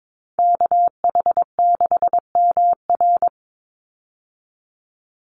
Morse code, heard as a keyed tone, about 22 wpm, 700 Hz. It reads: KH6MR
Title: USA Call Signs 22wpm